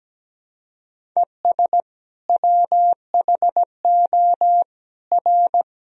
Morse code, heard as a keyed tone, about 17 wpm, 700 Hz. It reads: ES WHO R
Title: Sentences from Top 100 Words and Abbreviations 17wpm